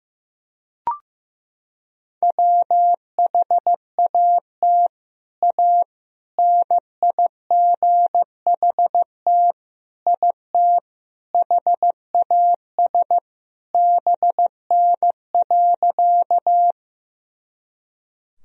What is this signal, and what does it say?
Morse code, heard as a keyed tone, about 15 wpm, 700 Hz. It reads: WHAT A NIGHT IT HAS BN.
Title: Sentences from Top 300 Words and Abbreviations 15wpm